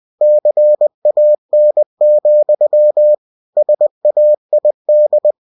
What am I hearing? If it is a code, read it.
Morse code, heard as a keyed tone, about 20 wpm, 600 Hz. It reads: CAN, SAID
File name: cw-books / winnie-the-pooh / winnie-20wpm-page-0046